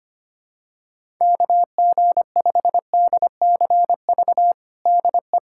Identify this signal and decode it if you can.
Morse code, heard as a keyed tone, about 25 wpm, 700 Hz. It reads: KG5DCV DE
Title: USA Callsign DE Callsign - 6 Characters - 25wpm